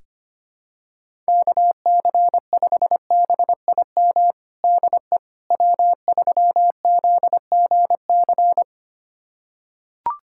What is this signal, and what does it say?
Morse code, heard as a keyed tone, about 25 wpm, 700 Hz. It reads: KC5BIM DE W3ZGC
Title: USA Callsign DE Callsign 25wpm